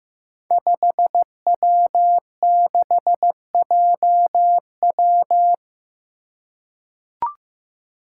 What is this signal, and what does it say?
Morse code, heard as a keyed tone, about 15 wpm, 700 Hz. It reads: 5W6JW
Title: Five Challenging Letter-Number Combinations 15wpm